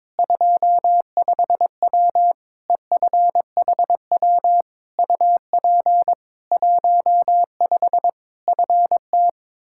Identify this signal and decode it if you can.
Morse code, heard as a keyed tone, about 22 wpm, 700 Hz. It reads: 25W EFHW UP 15 FT
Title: Practice QSO - Exchange 2 - Info - 22wpm